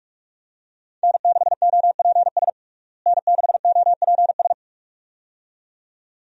Morse code, heard as a keyed tone, about 47 wpm, 700 Hz. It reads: N6OPS N6OPS
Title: USA Call Signs - 5 Characters - Repeated 2x - 45wpm